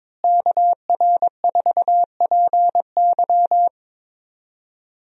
Morse code, heard as a keyed tone, about 22 wpm, 700 Hz. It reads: KR4PY
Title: USA Call Signs - 5 Characters - Repeated 2x - 22wpm